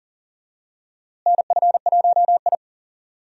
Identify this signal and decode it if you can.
Morse code, heard as a keyed tone, about 40 wpm, 700 Hz. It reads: NF1I
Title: USA Call Signs - 4 Characters - Repeated 2x - 40wpm